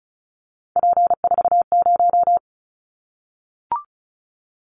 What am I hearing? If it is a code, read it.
Morse code, heard as a keyed tone, about 34 wpm, 700 Hz. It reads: P40
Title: Callsign Prefixes for Top 100 DXCC - 35wpm